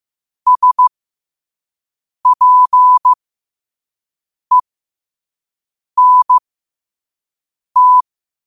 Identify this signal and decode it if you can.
Morse code, heard as a keyed tone, about 15 wpm, spaced as for 5 wpm, 1000 Hz. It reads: SPENT
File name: qotdfiles / 5wpm